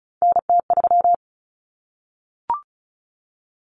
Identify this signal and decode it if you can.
Morse code, heard as a keyed tone, about 34 wpm, 700 Hz. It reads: NT3
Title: Common Cut Numbers - 3-Digit 35wpm